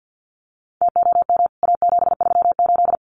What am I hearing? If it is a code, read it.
Morse code, heard as a keyed tone, about 50 wpm, 700 Hz. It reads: TOM A738
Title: CWT Exchange 50wpm